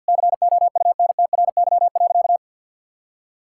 Morse code, heard as a keyed tone, about 50 wpm, 700 Hz. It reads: COUNTRY.
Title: Sentences from Top 1600 Words 50wpm